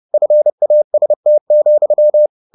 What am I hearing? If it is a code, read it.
Morse code, heard as a keyed tone, about 30 wpm, 600 Hz. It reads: FAST,